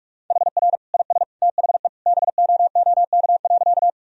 Morse code, heard as a keyed tone, about 41 wpm, 700 Hz. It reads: HR IS THE BOOK.
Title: Sentences from Top 300 Words and Abbreviations 45wpm